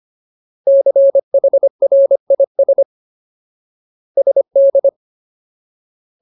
Morse code, heard as a keyed tone, about 25 wpm, 550 Hz. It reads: CHRIS SD
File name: Ultimate Word List / 25WPM Ultimate